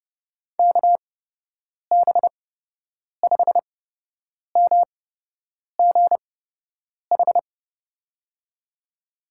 Morse code, heard as a keyed tone, about 30 wpm, spaced as for 8 wpm, 700 Hz. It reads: KB5MGH
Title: USA Call Signs - 8x Character Spacing - 30wpm